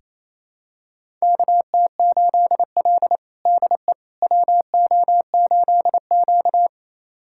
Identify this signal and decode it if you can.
Morse code, heard as a keyed tone, about 27 wpm, 700 Hz. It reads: KT8L DE WO8Q
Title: USA Callsign DE Callsign - 4 Characters - 28wpm